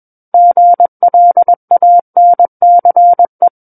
Morse code, heard as a keyed tone, about 21 wpm, 700 Hz. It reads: GLANCE
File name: Book One / 12-Chapter 12 - 21 WPM